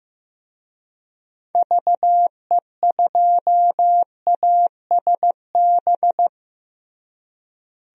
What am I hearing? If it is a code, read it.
Morse code, heard as a keyed tone, about 16 wpm, 700 Hz. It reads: VE2ASB
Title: DXCC-100 Call Signs - 6 Characters - Repeated 3x - 15wpm